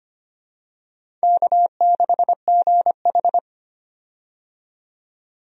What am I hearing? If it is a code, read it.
Morse code, heard as a keyed tone, about 25 wpm, 700 Hz. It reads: K6GH
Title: USA Call Signs - 4 Characters - 25wpm